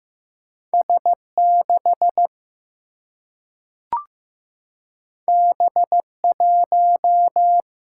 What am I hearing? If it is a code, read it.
Morse code, heard as a keyed tone, about 15 wpm, 700 Hz. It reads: S 6 B 1
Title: Single-Letter Single-Number 15wpm